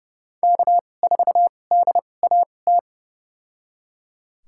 Morse code, heard as a keyed tone, about 30 wpm, spaced as for 22 wpm, 700 Hz. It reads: K4DAT
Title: USA Call Signs - 5 Characters - 2x Character Spacing - 30wpm